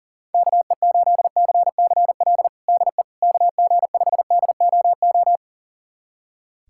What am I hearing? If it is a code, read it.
Morse code, heard as a keyed tone, about 40 wpm, 690 Hz. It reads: KE8CCL DE KG5DOO
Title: USA Callsign DE Callsign - 6 Characters - 40wpm